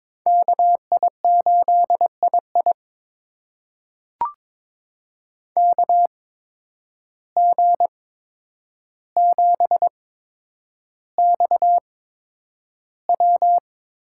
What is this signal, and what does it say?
Morse code, heard as a keyed tone, about 22 wpm, spaced as for 9 wpm, 700 Hz. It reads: KI8II KG7XW
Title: USA Call Signs - 5 Characters - 8x Character Spacing - 22wpm